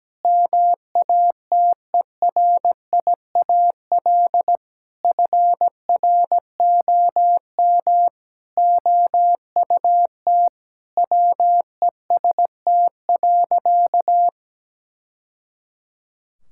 Morse code, heard as a keyed tone, about 17 wpm, 700 Hz. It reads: MATERIAL FROM OUT WEST.
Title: Sentences from Top 1000 Words - Review Part 5 - 17wpm